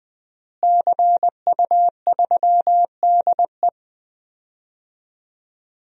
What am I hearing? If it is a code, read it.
Morse code, heard as a keyed tone, about 20 wpm, 700 Hz. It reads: CU3DE
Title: DXCC-100 Call Signs - 5 Characters - Repeated 3x - 20wpm